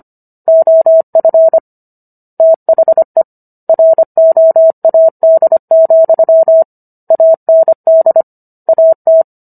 Morse code, heard as a keyed tone, about 25 wpm, 650 Hz. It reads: OF THE ROAD, AND AT